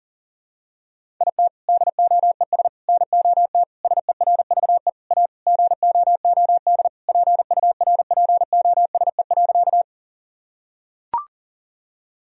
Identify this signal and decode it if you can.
Morse code, heard as a keyed tone, about 40 wpm, 700 Hz. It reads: IT DOES NOT SERVE A GOOD PURPOSE.
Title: Sentences from Top 1000 Words - Review Part 2 - 40wpm